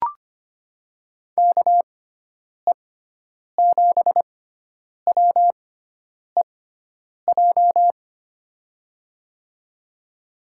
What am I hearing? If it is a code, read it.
Morse code, heard as a keyed tone, about 25 wpm, spaced as for 9 wpm, 700 Hz. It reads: KE7WEJ
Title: USA Call Signs - 6 Characters - 6x Character Spacing - 25wpm